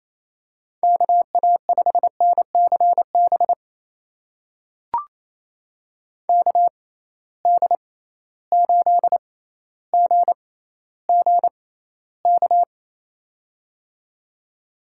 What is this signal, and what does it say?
Morse code, heard as a keyed tone, about 28 wpm, spaced as for 14 wpm, 700 Hz. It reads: KA5NCB KD8GGK
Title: USA Call Signs - 6 Characters - 6x Character Spacing - 28wpm